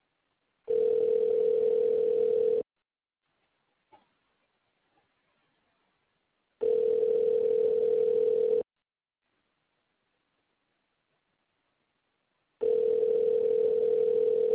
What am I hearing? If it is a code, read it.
Ringback tone (North America), 440 + 480 Hz.